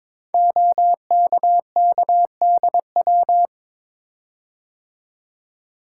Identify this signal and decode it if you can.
Morse code, heard as a keyed tone, about 22 wpm, 700 Hz. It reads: OKKDW
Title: Code Groups - 5 Letter:Numbers - 22wpm